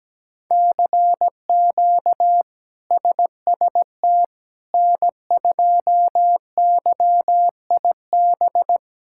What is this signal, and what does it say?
Morse code, heard as a keyed tone, about 17 wpm, 700 Hz. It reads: CQ SST N2YIB
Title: K1USN SST QSO Practice - 17wpm